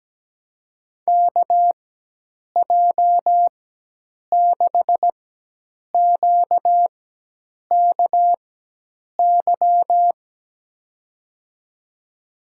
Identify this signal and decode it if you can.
Morse code, heard as a keyed tone, about 17 wpm, 700 Hz. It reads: K J 6 Q K Y
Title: USA Call Signs - 6 Characters - 4x Character Spacing - 17wpm